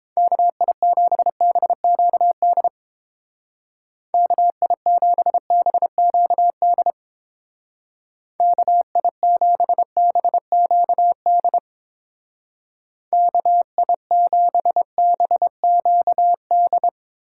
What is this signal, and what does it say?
Morse code, heard as a keyed tone, about 31 wpm, 700 Hz. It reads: KI7BQD KI7BQD KI7BQD KI7BQD